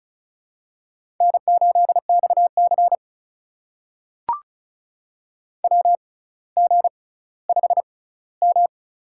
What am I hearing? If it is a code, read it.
Morse code, heard as a keyed tone, about 36 wpm, spaced as for 18 wpm, 700 Hz. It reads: N8XC WG5M
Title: USA Call Signs - 6x Character Spacing - 35wpm